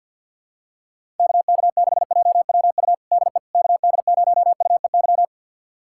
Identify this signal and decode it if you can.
Morse code, heard as a keyed tone, about 50 wpm, 700 Hz. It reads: KK6JWU DE KD0UEY
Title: USA Callsign DE Callsign 50wpm